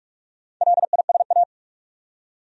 Morse code, heard as a keyed tone, about 46 wpm, 700 Hz. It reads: LISA
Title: USA Common Names - Max 4 Letters - 45wpm